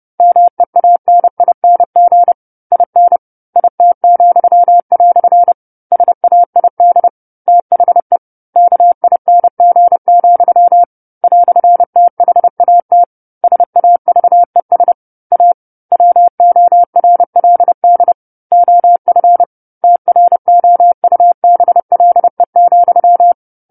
Morse code, heard as a keyed tone, about 30 wpm, 700 Hz. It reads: MEANING IN IT," SAID THE KING, "THAT SAVES A WORLD OF TROUBLE,
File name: 10-Chapter-10 30 WPM